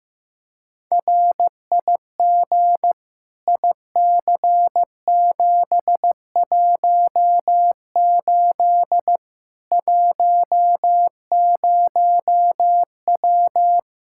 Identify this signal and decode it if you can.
Morse code, heard as a keyed tone, about 16 wpm, 700 Hz. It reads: RIG IC718 10W
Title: Practice QSO - Exchange 2 - Info - 15wpm